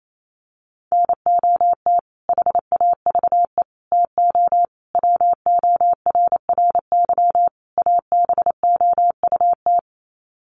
Morse code, heard as a keyed tone, about 28 wpm, 700 Hz. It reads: NOT HAVE TO WORRY ABOUT
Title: Sets of 5 Words 28wpm